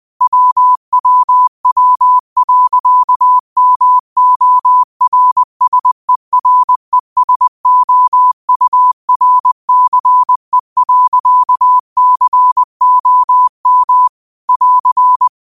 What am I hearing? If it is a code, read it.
Morse code, heard as a keyed tone, about 20 wpm, 1000 Hz. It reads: WWW.MORSERESOURCE.COM <AR>
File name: qotdfiles / 20wpm